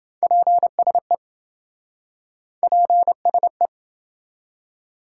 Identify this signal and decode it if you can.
Morse code, heard as a keyed tone, about 28 wpm, 700 Hz. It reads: PSE PSE